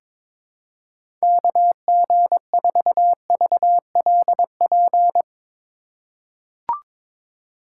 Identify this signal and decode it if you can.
Morse code, heard as a keyed tone, about 22 wpm, 700 Hz. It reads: KG4VLP
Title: USA Call Signs - 6 Characters - 3x Character Spacing - 22wpm